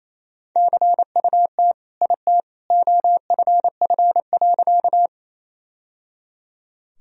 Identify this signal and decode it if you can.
Morse code, heard as a keyed tone, about 28 wpm, 700 Hz. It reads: CUT IT OFF.
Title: Sentences from Top 1000 Words - Review Part 1 - 28wpm